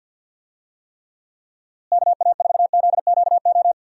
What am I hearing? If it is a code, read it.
Morse code, heard as a keyed tone, about 47 wpm, 700 Hz. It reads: KA4ZQO
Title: USA Call Signs - 6 Characters - Repeated 3x - 50wpm